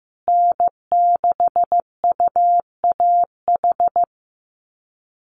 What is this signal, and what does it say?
Morse code, heard as a keyed tone, about 15 wpm, 700 Hz. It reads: N6UAH
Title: USA Call Signs Repeated 2x 15wpm